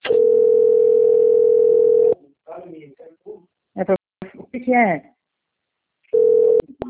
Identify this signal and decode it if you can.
Ringback tone (North America), 440 + 480 Hz, with speech.
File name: out- - - - - .25642